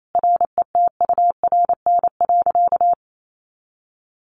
Morse code, heard as a keyed tone, about 28 wpm, 700 Hz. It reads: RETURN.
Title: A Sound Of Thunder 28wpm